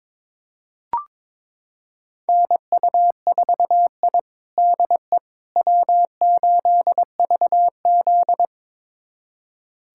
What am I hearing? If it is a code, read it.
Morse code, heard as a keyed tone, about 21 wpm, 700 Hz. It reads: NU4I DE W8VZ
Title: USA Callsign DE Callsign - 4 Characters - 22wpm